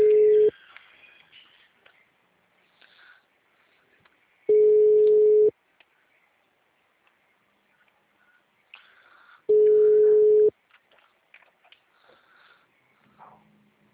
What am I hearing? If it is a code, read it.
Ringback tone (Europe, 425 Hz).
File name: out- -2001- - - .0